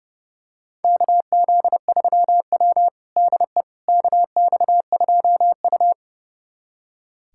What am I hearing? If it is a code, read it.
Morse code, heard as a keyed tone, about 30 wpm, 700 Hz. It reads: KZ3W DE KX2U
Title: USA Callsign DE Callsign - 4 Characters - 30wpm